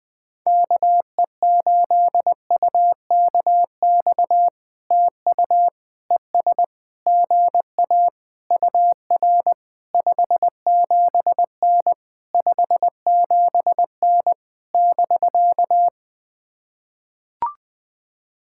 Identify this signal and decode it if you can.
Morse code, heard as a keyed tone, about 20 wpm, 700 Hz. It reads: KE8UKX TU ES GA UR 57N 57N <BK>
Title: POTA QSO Practice - 20wpm